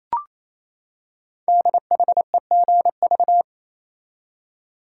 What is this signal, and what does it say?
Morse code, heard as a keyed tone, about 29 wpm, 700 Hz. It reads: DHEGV
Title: Code Groups - 5 Letters - 28wpm